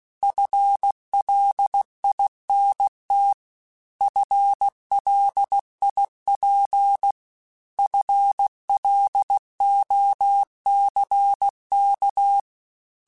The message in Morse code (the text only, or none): FLINT FLIP FLOCK